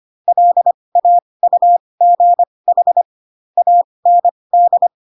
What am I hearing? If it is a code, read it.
Morse code, heard as a keyed tone, about 25 wpm, spaced as for 20 wpm, 700 Hz. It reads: LAUGH AND
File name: Pinocchio / Pinocchio ch1 15 25 ik1hge